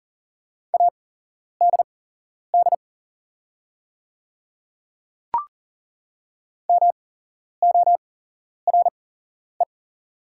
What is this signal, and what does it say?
Morse code, heard as a keyed tone, about 40 wpm, spaced as for 11 wpm, 700 Hz. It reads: ADD MORE
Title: 8x Character Spacing - Top 200 Words - 40wpm